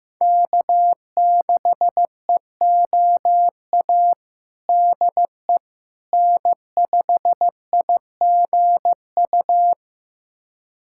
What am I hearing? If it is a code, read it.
Morse code, heard as a keyed tone, about 16 wpm, 700 Hz. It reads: K6EOA DE N5IGU
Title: USA Callsign DE Callsign - 5 Characters - 15wpm